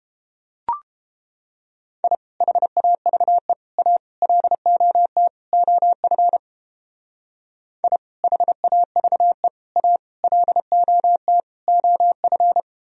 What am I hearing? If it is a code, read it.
Morse code, heard as a keyed tone, about 29 wpm, 700 Hz. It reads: I HAVE A LOT OF I HAVE A LOT OF